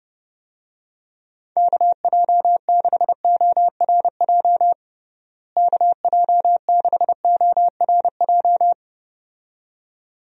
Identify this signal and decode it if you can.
Morse code, heard as a keyed tone, about 29 wpm, 700 Hz. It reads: KJ6ORJ KJ6ORJ
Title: USA Call Signs - 6 Characters - Repeated 2x - 30wpm